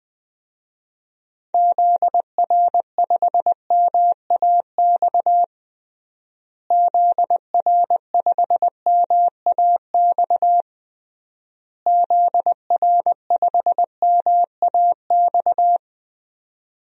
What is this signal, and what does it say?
Morse code, heard as a keyed tone, about 20 wpm, 700 Hz. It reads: ZR5MAX ZR5MAX ZR5MAX
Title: DXCC-100 Call Signs - 6 Characters - Repeated 3x - 20wpm